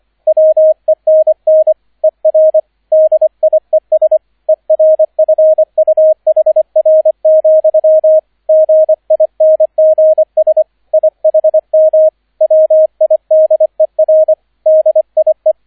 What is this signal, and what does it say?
Morse code, heard as a keyed tone, about 25 wpm, 630 Hz, with radio noise. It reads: WENN ER DIES ERFUHR, GINGS IHM WIDER DIE
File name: 003 Dritter Streich 25wpm